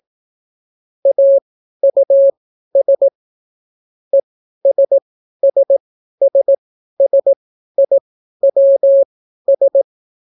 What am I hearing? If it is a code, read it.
Morse code, heard as a keyed tone, about 17 wpm, spaced as for 12 wpm, 550 Hz. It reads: AUS ESSSSIWS